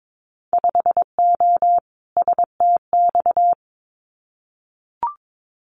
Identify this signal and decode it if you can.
Morse code, heard as a keyed tone, about 22 wpm, 700 Hz. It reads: 5O STX
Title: Winter Field Day Exchanges 22wpm